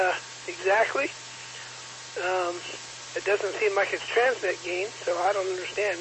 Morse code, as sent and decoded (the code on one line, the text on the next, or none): none